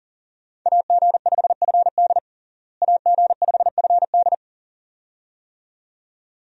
Morse code, heard as a keyed tone, about 40 wpm, 700 Hz. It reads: AG5FD AG5FD
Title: USA Call Signs - 5 Characters - Repeated 2x - 40wpm